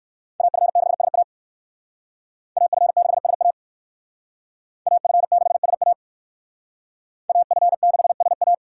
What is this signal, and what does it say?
Morse code, heard as a keyed tone, about 58 wpm, 700 Hz. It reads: AF6SA AF6SA AF6SA AF6SA